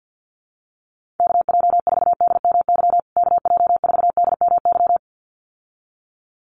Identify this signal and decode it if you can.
Morse code, heard as a keyed tone, about 49 wpm, 700 Hz. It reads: KJ4DMY KJ4DMY
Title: USA Call Signs Repeated 2x 50wpm